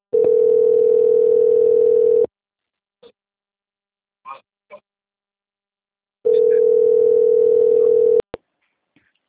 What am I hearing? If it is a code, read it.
Ringback tone (North America), 440 + 480 Hz.
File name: out- - - - - .58939